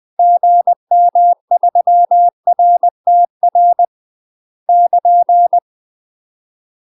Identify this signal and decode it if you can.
Morse code, heard as a keyed tone, about 20 wpm, 700 Hz. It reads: GM3RTR <KN>
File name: UK MASTER QSOs TRAINING / 20WPM / UK MASTER QSOs 0074